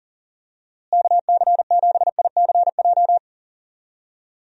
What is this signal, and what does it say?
Morse code, heard as a keyed tone, about 40 wpm, 700 Hz. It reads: KC7ICJ
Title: USA Call Signs - 6 Characters - Repeated 3x - 40wpm